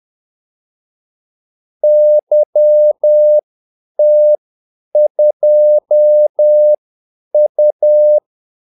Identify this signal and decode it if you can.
Morse code, heard as a keyed tone, about 10 wpm, spaced as for 8 wpm, 600 Hz. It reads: YT2U
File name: Callsigns-10-8wpm